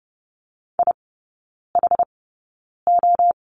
Morse code, heard as a keyed tone, about 30 wpm, spaced as for 9 wpm, 690 Hz. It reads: IHO
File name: Lesson 16e - Letter c - Three Characters 30wpm